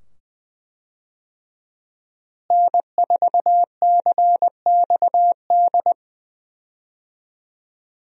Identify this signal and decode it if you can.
Morse code, heard as a keyed tone, about 20 wpm, 700 Hz. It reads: N4CXD
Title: USA Call Signs - 5 Characters - Repeated 2x - 20wpm